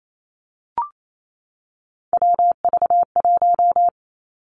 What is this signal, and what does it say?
Morse code, heard as a keyed tone, about 28 wpm, 700 Hz. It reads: WV1
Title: Three Challenging Letter-Number Combinations 28wpm